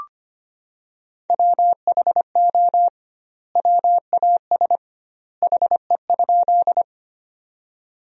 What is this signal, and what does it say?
Morse code, heard as a keyed tone, about 25 wpm, 700 Hz. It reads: WHO WAS HE?
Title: 2x Word-Spacing - Sentences from Top 100 Words - 25wpm